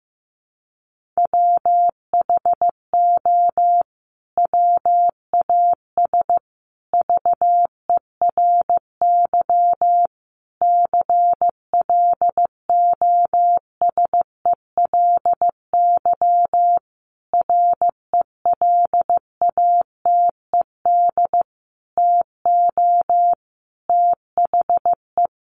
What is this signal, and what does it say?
Morse code, heard as a keyed tone, about 15 wpm, 700 Hz. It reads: WHO WAS VERY CLOSELY RELATED TO THE
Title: Sentences from Top 1400 Words 15wpm